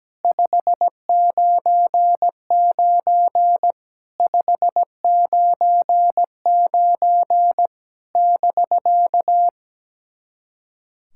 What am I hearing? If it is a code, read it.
Morse code, heard as a keyed tone, about 17 wpm, 700 Hz. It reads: 599 599 <BK>
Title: SOTA QSO Practice 17wpm